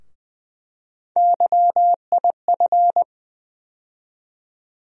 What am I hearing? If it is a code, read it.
Morse code, heard as a keyed tone, about 20 wpm, 700 Hz. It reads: YIF
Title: USA Call Sign Postfixes - 3 Letters 20wpm